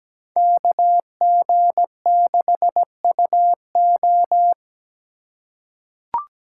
Morse code, heard as a keyed tone, about 17 wpm, 700 Hz. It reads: KG6UO
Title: USA Call Signs - 3x Character Spacing - 17wpm